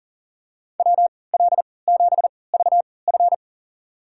Morse code, heard as a keyed tone, about 37 wpm, spaced as for 22 wpm, 700 Hz. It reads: WL7VF
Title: USA Call Signs - 5 Characters - 3x Character Spacing - 40wpm